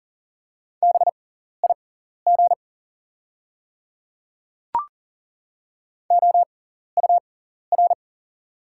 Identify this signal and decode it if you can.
Morse code, heard as a keyed tone, about 40 wpm, spaced as for 14 wpm, 700 Hz. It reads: BIG OUR
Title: 6x Character Spacing - Top 300 Words - 40wpm